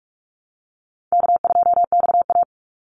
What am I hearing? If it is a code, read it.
Morse code, heard as a keyed tone, about 44 wpm, 700 Hz. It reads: K2XA
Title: Call Signs - 45wpm